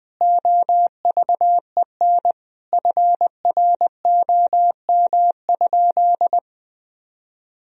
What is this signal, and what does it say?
Morse code, heard as a keyed tone, about 21 wpm, 700 Hz. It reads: OVEN FROM?
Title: Sentences from Top 1000 Words - Review Part 4 - 20wpm